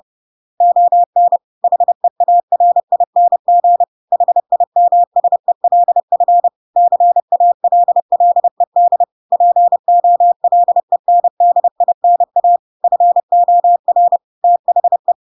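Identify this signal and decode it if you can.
Morse code, heard as a keyed tone, about 30 wpm, 700 Hz. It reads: ON HEARING HIMSELF CALLED POLENDINA FOR THE
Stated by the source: Chapter 2 of The Adventures of Pinocchio read as a book in Morse code (CW) at 30 wpm, by ik1hge.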